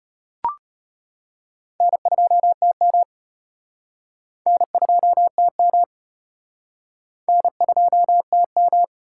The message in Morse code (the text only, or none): N2TM N2TM N2TM